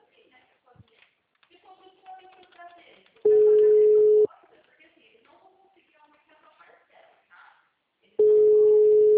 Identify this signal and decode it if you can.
Ringback tone (Europe, 425 Hz), with speech.